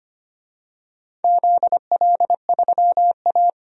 Morse code, heard as a keyed tone, about 25 wpm, 700 Hz. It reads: ZL3A
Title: DXCC-100 Call Signs - 4 Characters - 25wpm